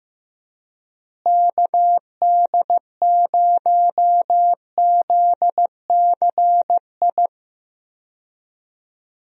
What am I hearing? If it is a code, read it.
Morse code, heard as a keyed tone, about 15 wpm, 700 Hz. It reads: KD0ZCI
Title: USA Call Signs - 6 Characters - Repeated 3x - 15wpm